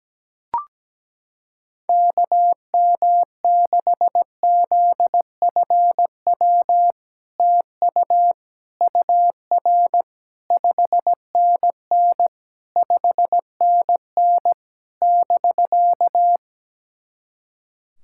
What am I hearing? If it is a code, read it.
Morse code, heard as a keyed tone, about 17 wpm, 700 Hz. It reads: KM6ZFW TU UR 5NN 5NN <BK>
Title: SOTA QSO Practice 17wpm